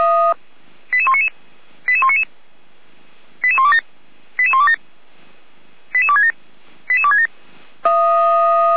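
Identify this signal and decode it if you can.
Five-tone selective call (ZVEI), heard as quick runs of five tones, tones of about 70 ms. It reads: 88199 88199 88127 88127 88267 88267
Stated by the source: ZVEI five-tone selective call received on an SDR, from the Feuerwehr Orlamünde fire brigade.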